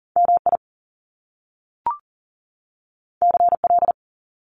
Morse code, heard as a keyed tone, about 38 wpm, 700 Hz. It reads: MI CL
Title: Top Two Letter Combinations in Words 40wpm